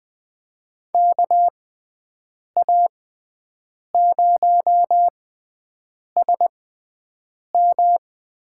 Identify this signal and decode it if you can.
Morse code, heard as a keyed tone, about 20 wpm, spaced as for 7 wpm, 700 Hz. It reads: KA0SM
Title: USA Call Signs - 6 Characters - 6x Character Spacing - 20wpm